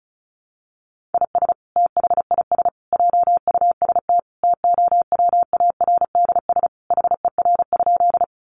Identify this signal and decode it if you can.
Morse code, heard as a keyed tone, about 34 wpm, 700 Hz. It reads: IS THIS JUST TOWARDS HER?
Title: Sentences from Top 1100 Words 35wpm